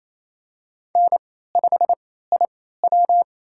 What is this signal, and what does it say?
Morse code, heard as a keyed tone, about 28 wpm, spaced as for 16 wpm, 700 Hz. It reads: N5IW
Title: USA Call Signs - 3x Character Spacing - 28wpm